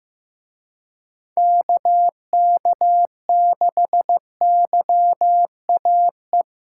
Morse code, heard as a keyed tone, about 17 wpm, 700 Hz. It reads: KK6YAE
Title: Speed Racing - USA 2x3 Call Signs 15wpm